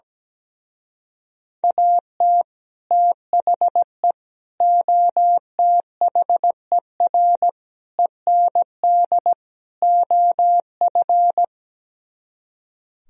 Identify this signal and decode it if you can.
Morse code, heard as a keyed tone, about 17 wpm, 700 Hz. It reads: AT THE OTHER END OF
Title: Speed-Racing - Sets of 5 Words 17wpm